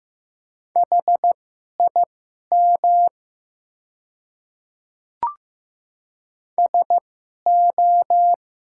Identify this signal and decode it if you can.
Morse code, heard as a keyed tone, about 15 wpm, spaced as for 11 wpm, 700 Hz. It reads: HIM SO
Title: 2x Character Spacing - Top 100 Words - 15wpm